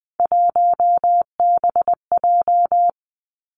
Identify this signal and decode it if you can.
Morse code, heard as a keyed tone, about 20 wpm, 700 Hz. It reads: 1BJ
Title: Three Challenging Letter-Number Combinations 20wpm